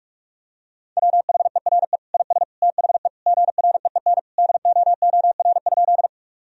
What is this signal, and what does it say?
Morse code, heard as a keyed tone, about 41 wpm, 700 Hz. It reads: WHERE IS THE GREEN DOOR?
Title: Sentences from Top 1000 Words - Review Part 2 - 45wpm